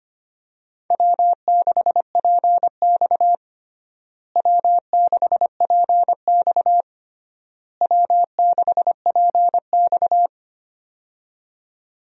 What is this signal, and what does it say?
Morse code, heard as a keyed tone, about 25 wpm, 700 Hz. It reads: W6PX W6PX W6PX
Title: USA Call Signs - 4 Characters - Repeated 3x - 25wpm